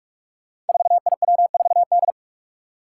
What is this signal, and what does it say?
Morse code, heard as a keyed tone, about 44 wpm, 700 Hz. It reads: 4IW4D
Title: Five Challenging Letter-Number Combinations 45wpm